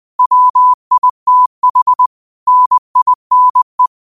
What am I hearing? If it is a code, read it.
Morse code, heard as a keyed tone, about 20 wpm, 1000 Hz. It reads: WITH NINE